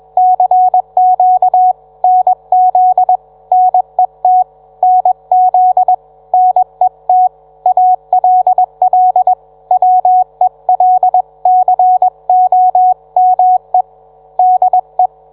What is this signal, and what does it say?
Morse code, heard as a keyed tone, about 21 wpm, 740 Hz. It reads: CQ NZ NET NZ NET ALL WELCOME DE